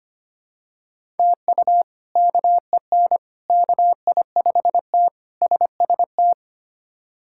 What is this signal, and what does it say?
Morse code, heard as a keyed tone, about 25 wpm, 700 Hz. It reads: TU KEN KI5T SST
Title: K1USN SST QSO Practice - 25wpm